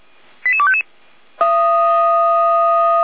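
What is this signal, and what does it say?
Five-tone selective call (ZVEI), heard as quick runs of five tones, tones of about 70 ms. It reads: 88255
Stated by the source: ZVEI five-tone call 88191 received by SDR.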